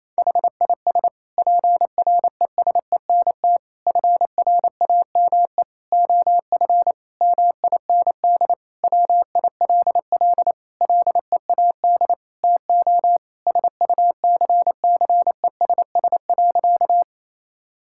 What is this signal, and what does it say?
Morse code, heard as a keyed tone, about 28 wpm, 700 Hz. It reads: HIS PRESENT FRAME OF MIND WILL LEAD TO SUCCESS.
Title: Sentences from Top 1000 Words - Review Part 6 - 28wpm